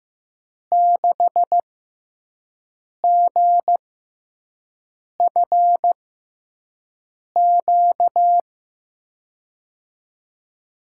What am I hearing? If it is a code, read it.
Morse code, heard as a keyed tone, about 15 wpm, spaced as for 5 wpm, 700 Hz. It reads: 6GFQ
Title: USA Call Signs - 6 Characters - 6x Character Spacing - 15wpm